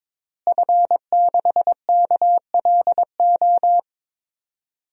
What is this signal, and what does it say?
Morse code, heard as a keyed tone, about 21 wpm, 700 Hz. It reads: F6KLO
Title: DXCC-100 Call Signs - 5 Characters - Repeated 3x - 22wpm